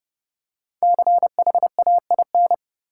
Morse code, heard as a keyed tone, about 30 wpm, 700 Hz. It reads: CHAIN
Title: Five Letter Words - 30wpm